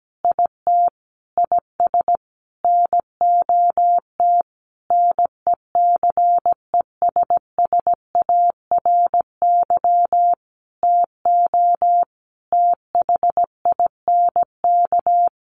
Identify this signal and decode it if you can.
Morse code, heard as a keyed tone, about 17 wpm, 700 Hz. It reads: IT IS NOT NECESSARY TO THINK
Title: Sentences from Top 1300 Words 17wpm